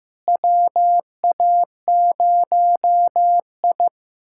Morse code, heard as a keyed tone, about 15 wpm, 700 Hz. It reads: WA0I
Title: USA Callsign DE Callsign - 4 Characters - 15wpm